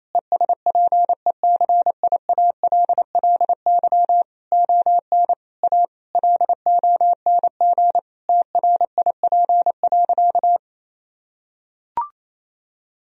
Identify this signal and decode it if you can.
Morse code, heard as a keyed tone, about 27 wpm, 700 Hz. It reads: ESPECIALLY ON A LONG TRIP.
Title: Sentences from Top 1000 Words - Review Part 4 - 28wpm